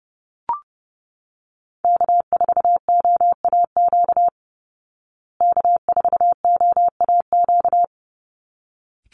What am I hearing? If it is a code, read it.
Morse code, heard as a keyed tone, about 30 wpm, 700 Hz. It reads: K4OAQ K4OAQ
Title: Call Signs Repeated 3x - 30wpm